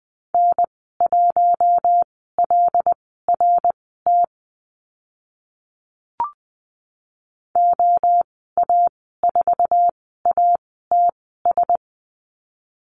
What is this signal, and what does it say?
Morse code, heard as a keyed tone, about 20 wpm, spaced as for 14 wpm, 700 Hz. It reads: N1LRT OA4ATS
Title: 076 Lesson 13g - Letter L - Call Signs 20wpm